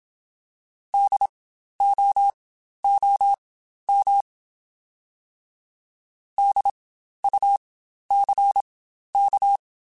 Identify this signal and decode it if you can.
Morse code, heard as a keyed tone, about 27 wpm, spaced as for 12 wpm, 800 Hz. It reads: DOOM DUCK
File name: Spaced Alphas / Spaced Alphas mp3s / Ds sp @25